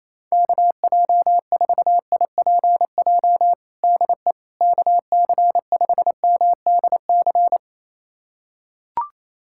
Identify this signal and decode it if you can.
Morse code, heard as a keyed tone, about 28 wpm, 700 Hz. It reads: KJ4IPJ DE KC5MDC
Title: USA Callsign DE Callsign - 6 Characters - 28wpm